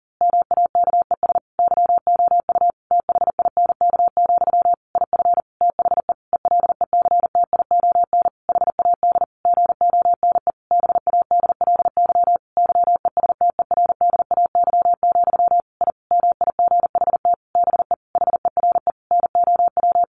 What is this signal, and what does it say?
Morse code, heard as a keyed tone, about 40 wpm, 700 Hz. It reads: MAKES YOU THINK, IF THE ELECTION HAD GONE BADLY YESTERDAY, I MIGHT BE HERE NOW
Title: A Sound Of Thunder 40wpm